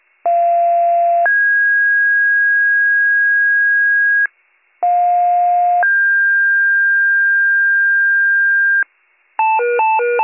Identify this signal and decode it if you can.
Two-tone page, heard as a one-second tone, then a three-second tone, about 690 then 1650 Hz, about 710 then 1600 Hz. It is heard from 0.3 to 8.8 s.